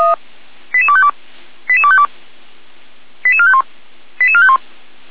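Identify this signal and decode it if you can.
Five-tone selective call (ZVEI), heard as quick runs of five tones, tones of about 70 ms. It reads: 88252 88252 88451 88451
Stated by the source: ZVEI five-tone selective call received on an SDR.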